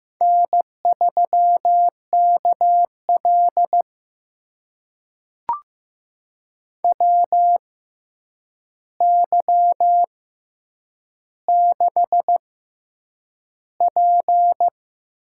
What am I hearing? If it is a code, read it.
Morse code, heard as a keyed tone, about 15 wpm, spaced as for 8 wpm, 700 Hz. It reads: N3KL WY6P
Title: USA Call Signs - 4 Characters - 6x Character Spacing - 15wpm